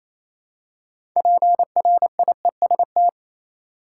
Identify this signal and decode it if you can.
Morse code, heard as a keyed tone, about 28 wpm, 700 Hz. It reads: PRIEST
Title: Six Letter Words - 28wpm